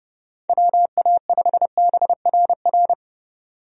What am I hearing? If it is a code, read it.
Morse code, heard as a keyed tone, about 31 wpm, 700 Hz. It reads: WA5BRR
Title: USA Call Signs - 6 Characters - 30wpm